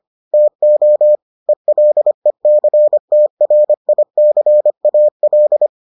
Morse code, heard as a keyed tone, about 25 wpm, 600 Hz. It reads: TO ELECTRICAL